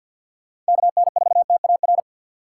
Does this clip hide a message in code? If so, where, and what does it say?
Yes: Morse code from 0.7 to 2.0 s, heard as a keyed tone, about 63 wpm, 700 Hz. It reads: KN4TAR